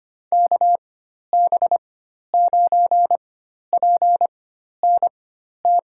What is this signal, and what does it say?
Morse code, heard as a keyed tone, about 25 wpm, spaced as for 12 wpm, 700 Hz. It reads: KB9PNT
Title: USA Call Signs - 6 Characters - 4x Character Spacing - 25wpm